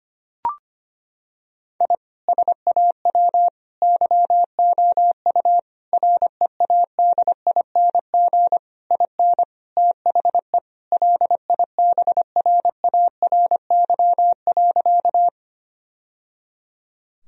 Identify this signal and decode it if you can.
Morse code, heard as a keyed tone, about 25 wpm, 700 Hz. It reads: I SAW YOU READING IN THE LIBRARY.
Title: Sentences from Top 1000 Words - Review Part 4 - 25wpm